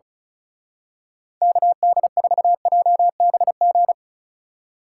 Morse code, heard as a keyed tone, about 33 wpm, 700 Hz. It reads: KD4JBG
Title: Speed Racing - USA 2x3 Call Signs 35wpm